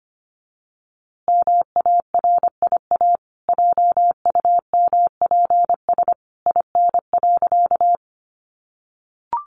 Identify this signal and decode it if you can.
Morse code, heard as a keyed tone, about 25 wpm, 700 Hz. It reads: MARIA JUMPS IN.
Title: Easy Phrases 25wpm